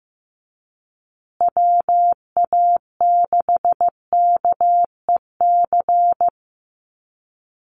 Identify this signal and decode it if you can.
Morse code, heard as a keyed tone, about 15 wpm, 700 Hz. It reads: WA6KEC
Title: USA Call Signs Repeated 2x 15wpm